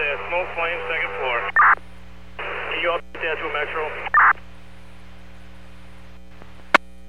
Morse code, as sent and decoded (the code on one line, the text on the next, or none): none